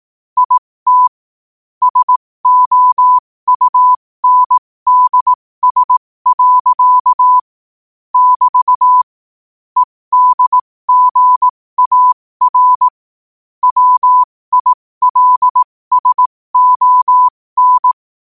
Morse code, heard as a keyed tone, about 18 wpm, spaced as for 15 wpm, 1000 Hz. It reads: IT SOUNDS. <BT> EDGAR WILSON
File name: qotdfiles / 15wpm